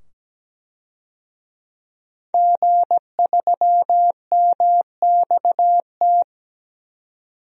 Morse code, heard as a keyed tone, about 17 wpm, 700 Hz. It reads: G3MXT
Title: DXCC-100 Call Signs - 5 Characters - 17wpm